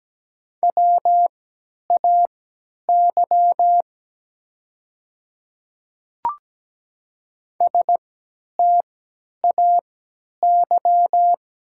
Morse code, heard as a keyed tone, about 17 wpm, spaced as for 10 wpm, 700 Hz. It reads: WAY STAY
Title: 3x Character Spacing - Top 500 Words - 17wpm